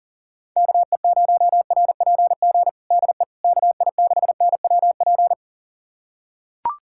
Morse code, heard as a keyed tone, about 40 wpm, 690 Hz. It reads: KE0RPG DE KI6NWP
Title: USA Callsign DE Callsign - 6 Characters - 40wpm